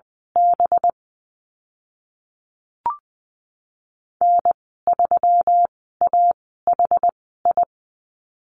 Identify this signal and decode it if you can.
Morse code, heard as a keyed tone, about 20 wpm, spaced as for 15 wpm, 700 Hz. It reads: B N3AHI
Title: 193 Lesson 31g - Letter v - Call Signs 20wpm